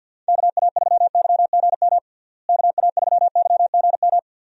Morse code, heard as a keyed tone, about 50 wpm, 700 Hz. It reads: KA3YGM KA3YGM
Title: USA Call Signs - 6 Characters - Repeated 3x - 50wpm